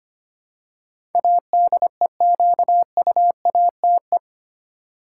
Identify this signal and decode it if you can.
Morse code, heard as a keyed tone, about 25 wpm, 700 Hz. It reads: ADEQUATE
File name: 8 Letter Words - First 3 Letters Given - 25wpm